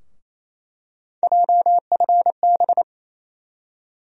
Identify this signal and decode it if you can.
Morse code, heard as a keyed tone, about 28 wpm, 700 Hz. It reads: JFB
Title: USA Call Sign Postfixes - 3 Letters 28wpm